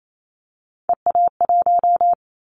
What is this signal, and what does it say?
Morse code, heard as a keyed tone, about 28 wpm, 700 Hz. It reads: EA1
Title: Callsign Prefixes for Top 25 DXCC - 28wpm